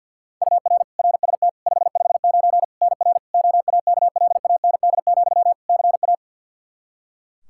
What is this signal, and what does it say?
Morse code, heard as a keyed tone, about 52 wpm, 700 Hz. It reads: UR RST 559 NR OAKLAND, CA
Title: K1USN SST QSO Practice 50wpm